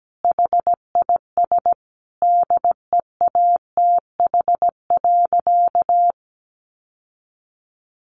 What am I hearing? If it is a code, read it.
Morse code, heard as a keyed tone, about 17 wpm, 700 Hz. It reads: HIS DEATH.
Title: Sentences from Top 1100 Words 17wpm